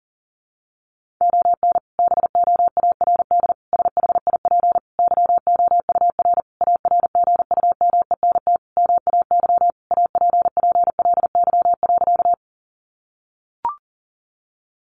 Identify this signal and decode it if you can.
Morse code, heard as a keyed tone, about 37 wpm, 700 Hz. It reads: ON BOARD SHIP YOUR ARGUMENT MAY APPLY.
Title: Sentences from Top 1400 Words 40wpm